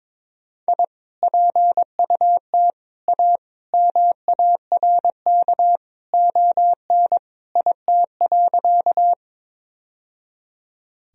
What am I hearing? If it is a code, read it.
Morse code, heard as a keyed tone, about 22 wpm, 700 Hz. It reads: I PUT A MARK ON IT.
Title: Sentences from Top 1000 Words - Review Part 1 - 22wpm